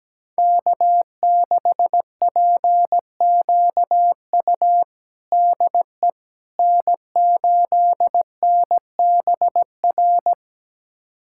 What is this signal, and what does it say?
Morse code, heard as a keyed tone, about 17 wpm, 700 Hz. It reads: K6PQU DE N8NBR
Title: USA Callsign DE Callsign - 5 Characters - 17wpm